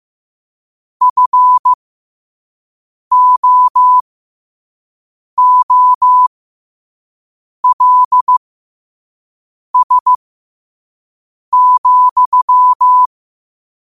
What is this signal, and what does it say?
Morse code, heard as a keyed tone, about 15 wpm, spaced as for 5 wpm, 1000 Hz. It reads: FOOLS,
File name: qotdfiles / 5wpm